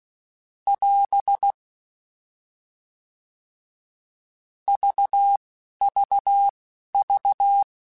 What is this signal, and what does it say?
Morse code, heard as a keyed tone, about 16 wpm, spaced as for 11 wpm, 800 Hz. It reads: <AS> VVV
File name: Short Runs / Short Runs-mp3s / GZ GZ7 @10